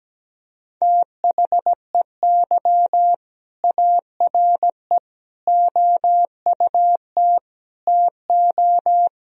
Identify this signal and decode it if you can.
Morse code, heard as a keyed tone, about 17 wpm, 700 Hz. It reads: THEY ARE OUT TO
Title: Sentences from Top 1000 Words - Review Part 1 - 17wpm